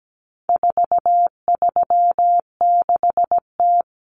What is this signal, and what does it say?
Morse code, heard as a keyed tone, about 17 wpm, 700 Hz. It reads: 436T
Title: Common Cut Numbers - 4-Digit 17wpm